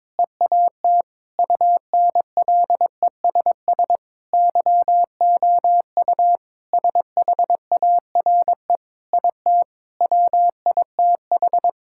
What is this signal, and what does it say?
Morse code, heard as a keyed tone, about 22 wpm, 700 Hz. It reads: EAT UNLESS YOU SHARE IT WITH
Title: Sentences from Top 1000 Words - Review Part 6 - 22wpm